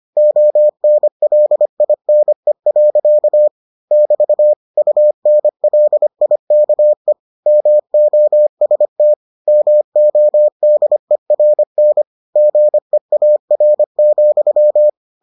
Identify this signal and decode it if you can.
Morse code, heard as a keyed tone, about 25 wpm, 600 Hz. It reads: ONLINE. <BT> UNLIKE MOST MODERN GEAR,